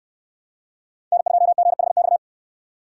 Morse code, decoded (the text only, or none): N2GHX